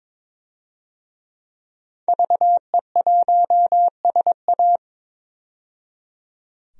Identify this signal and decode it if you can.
Morse code, heard as a keyed tone, about 23 wpm, 700 Hz. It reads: VE1SA
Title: DXCC-100 Call Signs - 5 Characters - 22wpm